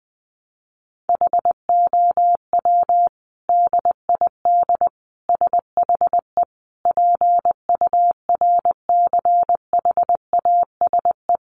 Morse code, heard as a keyed tone, about 19 wpm, 700 Hz. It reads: HOW DID SHE PURCHASE
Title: Sentences from Top 800 Words 20wpm